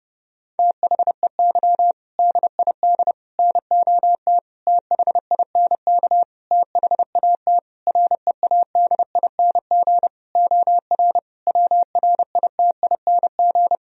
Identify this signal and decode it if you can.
Morse code, heard as a keyed tone, about 30 wpm, 700 Hz. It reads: THEY DID NOT THINK THAT READING OR WRITING
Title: Sentences from Top 1000 Words - Review Part 3 - 30wpm